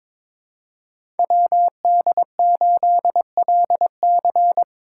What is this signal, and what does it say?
Morse code, heard as a keyed tone, about 26 wpm, 700 Hz. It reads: WD8LC
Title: Speed Racing - USA 2x2 Call Signs 22wpm